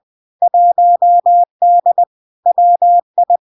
Morse code, heard as a keyed tone, about 20 wpm, 700 Hz. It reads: 1D WI
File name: LARK-FD-2022-Log-in-CW-20wpm-700-Hz